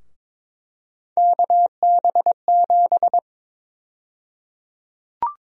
Morse code, heard as a keyed tone, about 22 wpm, 700 Hz. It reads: KB7
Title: USA Call Sign Prefixes - Repeated 2x - 22wpm